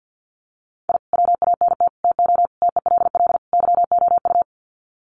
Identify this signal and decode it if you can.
Morse code, heard as a keyed tone, about 50 wpm, 700 Hz. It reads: I WANT TO TELL YOU
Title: Sets of 5 Words Encore 50wpm